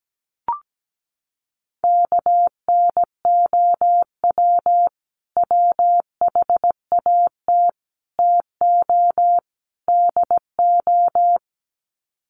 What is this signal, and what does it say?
Morse code, heard as a keyed tone, about 17 wpm, 700 Hz. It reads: KNOW WHAT TO DO
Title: Sets of 4 Words 17wpm